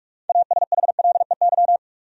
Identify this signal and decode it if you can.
Morse code, heard as a keyed tone, about 47 wpm, 700 Hz. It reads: ASHLEY
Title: USA Common Names - 45wpm